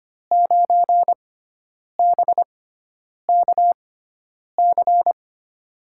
Morse code, heard as a keyed tone, about 25 wpm, spaced as for 9 wpm, 700 Hz. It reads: 9BKC
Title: USA Call Signs - 6 Characters - 6x Character Spacing - 25wpm